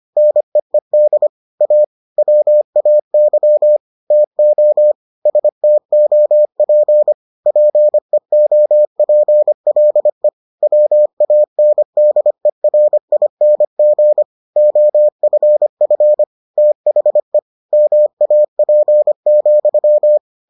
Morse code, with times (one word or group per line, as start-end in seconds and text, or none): 0.2-1.3 NEED
1.6-1.8 A
2.2-3.8 WAY
4.1-4.9 TO
5.3-7.1 STOP
7.5-10.3 PEOPLE
10.6-14.2 WANDERING
14.6-16.2 OFF
16.6-17.4 THE
17.7-20.2 MAP,